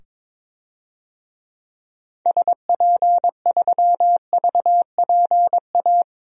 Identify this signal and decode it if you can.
Morse code, heard as a keyed tone, about 22 wpm, 700 Hz. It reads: SP3VPA
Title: DXCC-100 Call Signs - 6 Characters - 22wpm